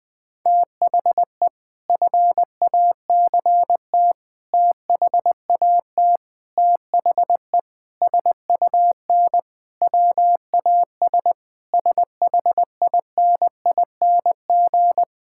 Morse code, heard as a keyed tone, about 20 wpm, 700 Hz. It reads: THE FACT THAT THE SUN WAS SHINING
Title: Sentences with 4-Word Phrases - 20wpm